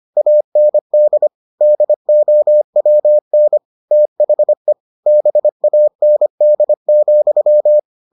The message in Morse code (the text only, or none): AND DOWN THE BAND,